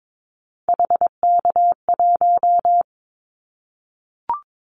Morse code, heard as a keyed tone, about 22 wpm, 700 Hz. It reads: HK1
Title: Callsign Prefixes for Top 100 DXCC - 22wpm